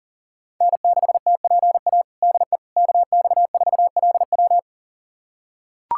Morse code, heard as a keyed tone, about 40 wpm, 700 Hz. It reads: N6TPA DE KX4LW
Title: USA Callsign DE Callsign - 5 Characters - 40wpm